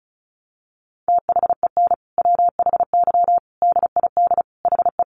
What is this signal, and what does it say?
Morse code, heard as a keyed tone, about 34 wpm, 700 Hz. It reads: THEN WHY DID HE
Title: Sentences from Top 800 Words 35wpm